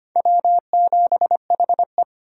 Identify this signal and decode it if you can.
Morse code, heard as a keyed tone, about 25 wpm, 700 Hz. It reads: W7HE
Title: USA Call Signs - 4 Characters - 6x Character Spacing - 25wpm